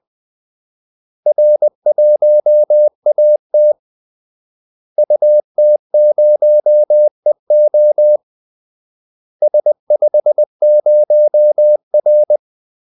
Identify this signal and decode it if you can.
Morse code, heard as a keyed tone, about 20 wpm, 600 Hz. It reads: R1AT UT0EO S50R
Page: Kaldesignaler 20 wpm